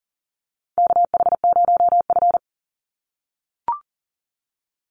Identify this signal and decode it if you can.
Morse code, heard as a keyed tone, about 40 wpm, 700 Hz. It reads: KH0F
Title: USA Call Signs 40wpm